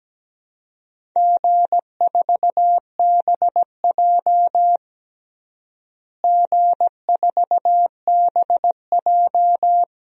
Morse code, heard as a keyed tone, about 17 wpm, 700 Hz. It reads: G4BJ G4BJ
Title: DXCC-100 Call Signs - 4 Characters - Repeated 2x - 17wpm